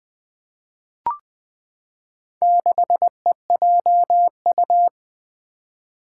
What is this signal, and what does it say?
Morse code, heard as a keyed tone, about 20 wpm, 700 Hz. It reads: 6EJU
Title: Four Challenging Letter-Number Combinations 20wpm